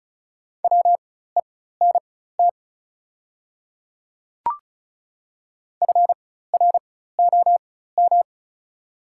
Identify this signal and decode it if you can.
Morse code, heard as a keyed tone, about 36 wpm, spaced as for 16 wpm, 700 Hz. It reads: WENT FROM
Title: 4x Character Spacing - Top 200 Words - 35wpm